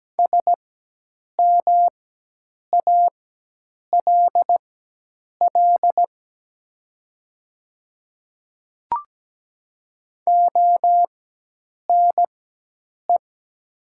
Morse code, heard as a keyed tone, about 17 wpm, spaced as for 8 wpm, 700 Hz. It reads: SMALL ONE
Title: 4x Character Spacing - Top 200 Words - 17wpm